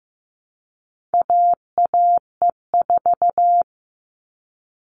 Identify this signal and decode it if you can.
Morse code, heard as a keyed tone, about 15 wpm, 700 Hz. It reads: AAE4
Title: All Cut Numbers - 4-Digit 15wpm